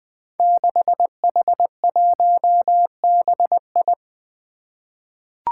Morse code, heard as a keyed tone, about 20 wpm, 700 Hz. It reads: 6H1BI
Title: Five Challenging Letter-Number Combinations 20wpm